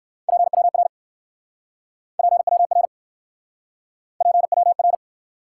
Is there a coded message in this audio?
Yes: Morse code from 0.3 to 5.0 s, heard as a keyed tone, about 59 wpm, 700 Hz. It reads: PWR PWR PWR